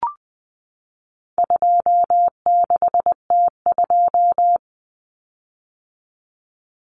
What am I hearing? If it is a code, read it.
Morse code, heard as a keyed tone, about 20 wpm, 700 Hz. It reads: 26T2
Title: Common Cut Numbers - 4-Digit 20wpm